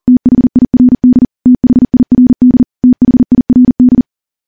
Morse code, heard as a keyed tone, about 42 wpm, 250 Hz. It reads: THIRD THIRD THIRD
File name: morse-training-40WPM